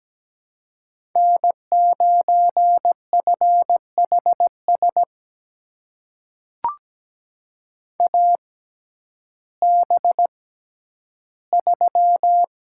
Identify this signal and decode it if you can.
Morse code, heard as a keyed tone, about 17 wpm, spaced as for 10 wpm, 700 Hz. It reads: N9FHS AB3
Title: USA Call Signs - 5 Characters - 6x Character Spacing - 17wpm